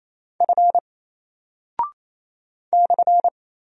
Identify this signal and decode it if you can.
Morse code, heard as a keyed tone, about 28 wpm, 700 Hz. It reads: F /
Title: Single Character (Letter:Number:Punctuation:Prosign) 28wpm